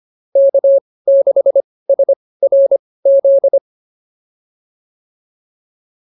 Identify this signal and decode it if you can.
Morse code, heard as a keyed tone, about 25 wpm, spaced as for 18 wpm, 550 Hz. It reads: K6SRZ
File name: POTA109 18